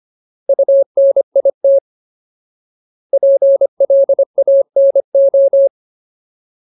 Morse code, heard as a keyed tone, about 25 wpm, 550 Hz. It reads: UNIT PLANO
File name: Ultimate Word List / 25WPM Ultimate